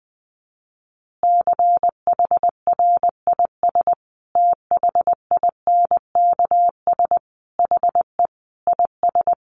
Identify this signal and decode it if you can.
Morse code, heard as a keyed tone, about 19 wpm, 700 Hz. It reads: CHRIS THINKS HE IS
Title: Additional Easy Phrases 20wpm